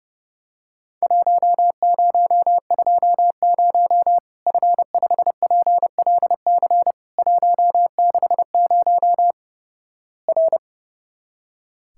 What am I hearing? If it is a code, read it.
Morse code, heard as a keyed tone, about 32 wpm, 700 Hz. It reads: 1020 F5PLC 160 R
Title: Worked All Europe DX Contest - QTC Exchange - 30wpm